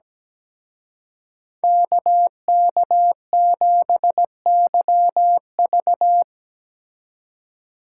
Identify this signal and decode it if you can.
Morse code, heard as a keyed tone, about 18 wpm, 700 Hz. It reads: KK7YV
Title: Speed Racing - USA 2x2 Call Signs 17wpm